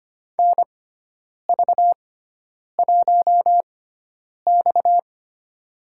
Morse code, heard as a keyed tone, about 25 wpm, spaced as for 9 wpm, 700 Hz. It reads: NV1X
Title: USA Call Signs - 6x Character Spacing - 25wpm